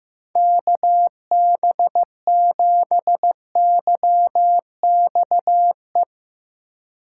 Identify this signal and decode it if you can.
Morse code, heard as a keyed tone, about 15 wpm, 700 Hz. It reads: KB7YXE
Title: USA Call Signs - 6 Characters - 2x Character Spacing - 15wpm